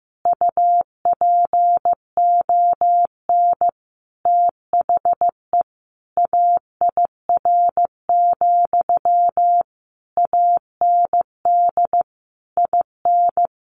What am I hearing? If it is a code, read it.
Morse code, heard as a keyed tone, about 15 wpm, 700 Hz. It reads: UPON THE AIR, AND IN
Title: A Sound Of Thunder 15wpm